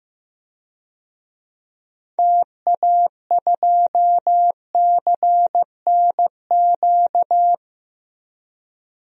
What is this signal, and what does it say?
Morse code, heard as a keyed tone, about 16 wpm, 700 Hz. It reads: TA2CNQ
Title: DXCC-100 Call Signs - 6 Characters - 15wpm